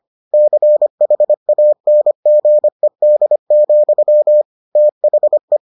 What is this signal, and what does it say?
Morse code, heard as a keyed tone, about 25 wpm, 600 Hz. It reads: CHANGED, THE